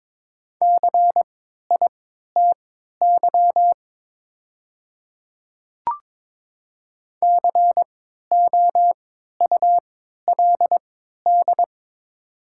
Morse code, heard as a keyed tone, about 22 wpm, spaced as for 12 wpm, 690 Hz. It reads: CITY COULD
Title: 3x Character Spacing - Top 300 Words - 22wpm